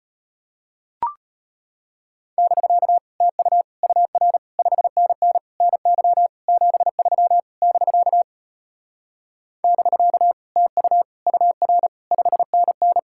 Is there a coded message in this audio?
Yes: Morse code from 2.4 to 13.0 s, heard as a keyed tone, about 33 wpm, 700 Hz. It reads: <BK> TU UR 5NN NY 73 <BK> <BK> TU UR 5NN